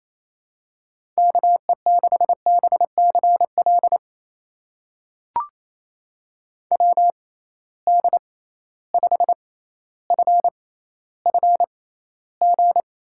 Morse code, heard as a keyed tone, about 28 wpm, spaced as for 14 wpm, 700 Hz. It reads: KE6BCL WD5FFG
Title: USA Call Signs - 6 Characters - 6x Character Spacing - 28wpm